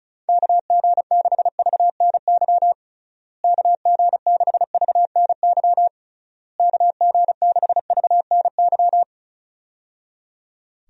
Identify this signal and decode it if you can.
Morse code, heard as a keyed tone, about 36 wpm, 700 Hz. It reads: KG6VNY KG6VNY KG6VNY
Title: USA Call Signs - 6 Characters - Repeated 3x - 35wpm